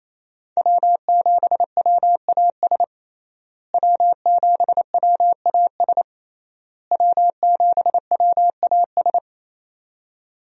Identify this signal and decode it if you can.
Morse code, heard as a keyed tone, about 28 wpm, 700 Hz. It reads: W7WAS W7WAS W7WAS
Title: USA Call Signs - 5 Characters - Repeated 3x - 28wpm